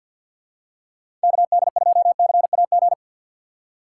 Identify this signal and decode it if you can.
Morse code, heard as a keyed tone, about 53 wpm, 700 Hz. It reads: KD2CAG
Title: USA Call Signs - 6x Character Spacing - 50wpm